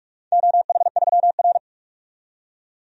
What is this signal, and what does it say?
Morse code, heard as a keyed tone, about 46 wpm, 700 Hz. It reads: OH3R
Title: DXCC-100 Call Signs - 4 Characters - 45wpm